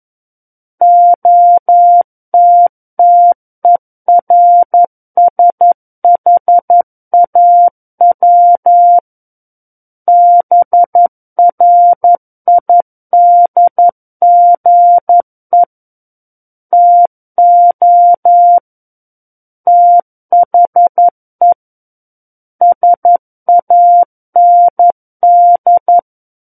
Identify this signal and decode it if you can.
Morse code, heard as a keyed tone, about 11 wpm, 700 Hz. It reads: OTTERSHAW BRIDGE TO THE SAND
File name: 11 WPM